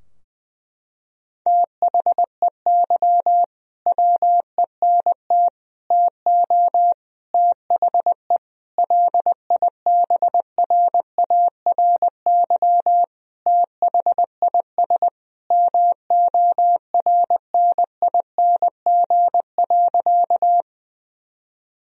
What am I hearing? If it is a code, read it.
Morse code, heard as a keyed tone, about 20 wpm, 700 Hz. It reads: THEY WENT TO THE LIBRARY THIS MORNING.
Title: Sentences from Top 1000 Words - Review Part 4 - 20wpm